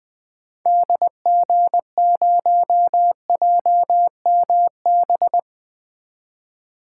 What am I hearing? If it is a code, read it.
Morse code, heard as a keyed tone, about 20 wpm, 700 Hz. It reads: DG0JMB
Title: 2023 SOTA Activator Call Signs - 20wpm